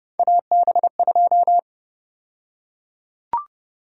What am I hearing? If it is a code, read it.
Morse code, heard as a keyed tone, about 30 wpm, 700 Hz. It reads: AB2
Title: USA Call Sign Prefixes - 30wpm